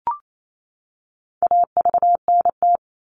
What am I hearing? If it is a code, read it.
Morse code, heard as a keyed tone, about 28 wpm, 700 Hz. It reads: AVNT
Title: All Cut Numbers - 4-Digit 28wpm